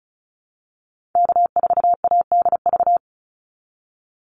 Morse code, heard as a keyed tone, about 34 wpm, 700 Hz. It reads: K4ADV
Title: NA SOTA Activator Callsigns 35wpm